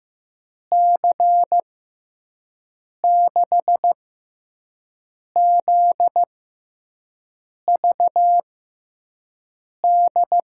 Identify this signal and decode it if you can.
Morse code, heard as a keyed tone, about 15 wpm, spaced as for 5 wpm, 700 Hz. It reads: C6ZVD
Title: USA Call Signs - 6 Characters - 6x Character Spacing - 15wpm